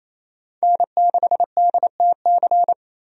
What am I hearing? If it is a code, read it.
Morse code, heard as a keyed tone, about 28 wpm, 700 Hz. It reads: N6DTC
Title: USA Call Signs - 5 Characters - 28wpm